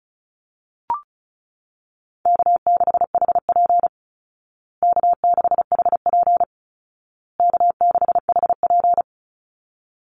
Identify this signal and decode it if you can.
Morse code, heard as a keyed tone, about 34 wpm, 700 Hz. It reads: K6HP K6HP K6HP
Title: Easy Call Signs Repeated 3x - 35wpm